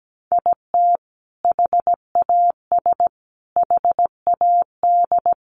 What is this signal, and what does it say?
Morse code, heard as a keyed tone, about 17 wpm, 700 Hz. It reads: IT HAS HAD
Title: Sentences from Top 1100 Words 17wpm